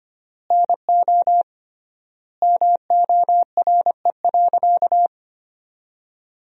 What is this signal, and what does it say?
Morse code, heard as a keyed tone, about 25 wpm, 700 Hz. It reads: NO MORE.
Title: 3x Word-Spacing - Sentences from Top 100 Words - 25wpm